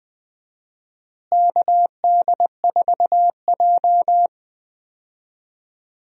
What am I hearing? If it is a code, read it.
Morse code, heard as a keyed tone, about 20 wpm, 700 Hz. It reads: KD4J
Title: USA Call Signs - 4 Characters - Repeated 2x - 20wpm